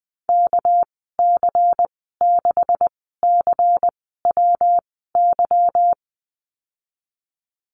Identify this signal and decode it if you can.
Morse code, heard as a keyed tone, about 20 wpm, spaced as for 15 wpm, 700 Hz. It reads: KC6CWY
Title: 213 Lesson 34g - Number 8 - Call Signs 20wpm